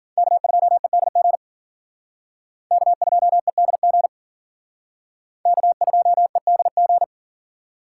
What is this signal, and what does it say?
Morse code, heard as a keyed tone, about 49 wpm, 700 Hz. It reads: K2EDG K2EDG K2EDG
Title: Speed Racing - USA 1x3 Call Signs 40wpm